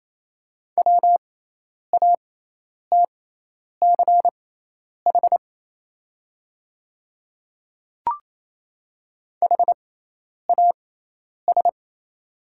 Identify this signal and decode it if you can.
Morse code, heard as a keyed tone, about 28 wpm, spaced as for 10 wpm, 690 Hz. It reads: WATCH HAS
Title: 6x Character Spacing - Top 500 Words - 28wpm